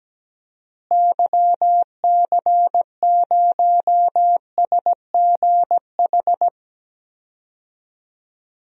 Morse code, heard as a keyed tone, about 17 wpm, 700 Hz. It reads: YC0SGH
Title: DXCC-100 Call Signs - 6 Characters - Repeated 3x - 17wpm